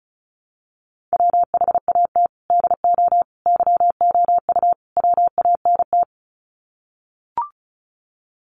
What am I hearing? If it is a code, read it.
Morse code, heard as a keyed tone, about 34 wpm, 700 Hz. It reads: WHAT DO YOU WANT
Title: Sets of 4 Words 35wpm